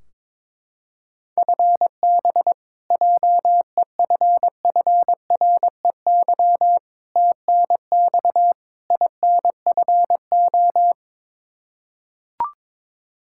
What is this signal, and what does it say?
Morse code, heard as a keyed tone, about 21 wpm, 700 Hz. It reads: FB JEFFREY TNX INFO
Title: Practice QSO - Exchange 3 - More Info - 22wpm